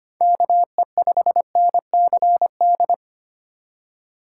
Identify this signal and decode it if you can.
Morse code, heard as a keyed tone, about 25 wpm, 700 Hz. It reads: KE5NCD
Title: USA Call Signs - 6 Characters - 3x Character Spacing - 25wpm